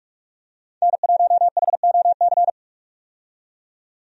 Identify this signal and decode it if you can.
Morse code, heard as a keyed tone, about 46 wpm, 700 Hz. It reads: N1HOC
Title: USA Call Signs - 5 Characters - 45wpm